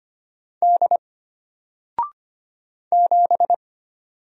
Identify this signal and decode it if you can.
Morse code, heard as a keyed tone, about 25 wpm, 700 Hz. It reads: D 7
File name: Single-Letter-Number-25wpm